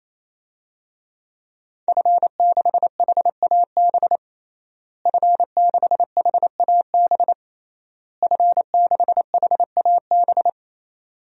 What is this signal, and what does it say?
Morse code, heard as a keyed tone, about 27 wpm, 700 Hz. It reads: F6HAB F6HAB F6HAB
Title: DXCC-100 Call Signs - 5 Characters - Repeated 3x - 28wpm